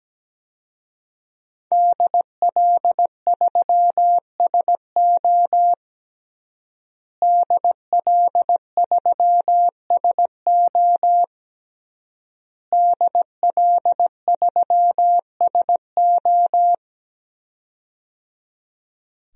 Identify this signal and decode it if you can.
Morse code, heard as a keyed tone, about 17 wpm, 700 Hz. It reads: DL3SO DL3SO DL3SO
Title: DXCC-100 Call Signs - 5 Characters - Repeated 3x - 17wpm